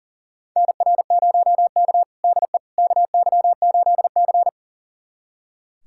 Morse code, heard as a keyed tone, about 37 wpm, 700 Hz. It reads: NR0K DE KY8C
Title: USA Callsign DE Callsign - 4 Characters - 40wpm